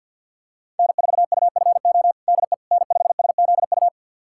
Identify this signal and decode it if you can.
Morse code, heard as a keyed tone, about 48 wpm, 700 Hz. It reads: N4UFO DE N5SZU
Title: USA Callsign DE Callsign - 5 Characters - 50wpm